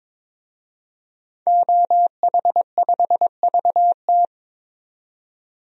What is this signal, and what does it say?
Morse code, heard as a keyed tone, about 21 wpm, 700 Hz. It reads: OH5VT
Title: DXCC-100 Call Signs - 5 Characters - 22wpm